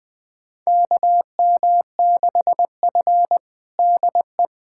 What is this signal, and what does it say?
Morse code, heard as a keyed tone, about 20 wpm, 700 Hz. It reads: KM6F DE
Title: USA Callsign DE Callsign - 4 Characters - 20wpm